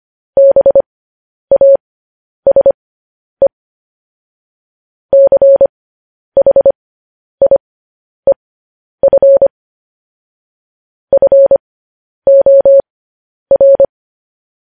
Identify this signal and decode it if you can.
Morse code, heard as a keyed tone, about 25 wpm, spaced as for 10 wpm, 550 Hz. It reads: BASE CHIEF FOR
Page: news headlines in morse code at 10 wpm